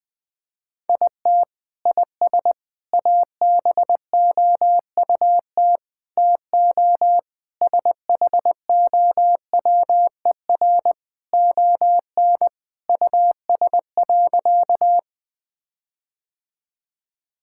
Morse code, heard as a keyed tone, about 21 wpm, 700 Hz. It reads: IT IS ABOUT TO SHOWER ON US.
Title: Sentences from Top 1600 Words 20wpm